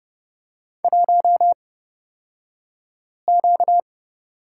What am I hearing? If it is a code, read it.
Morse code, heard as a keyed tone, about 30 wpm, spaced as for 5 wpm, 700 Hz. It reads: 1Q
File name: Lesson 38d - Letter q - Two Characters 30wpm